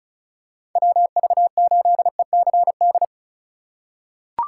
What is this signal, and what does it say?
Morse code, heard as a keyed tone, about 36 wpm, 700 Hz. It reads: WV8ECD
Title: USA Call Signs - 6 Characters - 3x Character Spacing - 35wpm